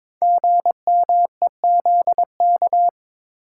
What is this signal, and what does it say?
Morse code, heard as a keyed tone, about 22 wpm, 700 Hz. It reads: GMEZK
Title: Code Groups - 5 Letter:Numbers - 22wpm